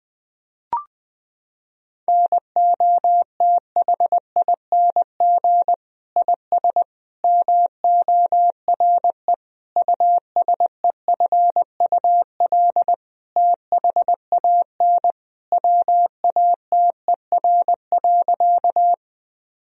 Morse code, heard as a keyed tone, about 21 wpm, 700 Hz. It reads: NOTHING IS MORE USEFUL THAN WATER.
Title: Sentences from Top 1000 Words - Review Part 4 - 20wpm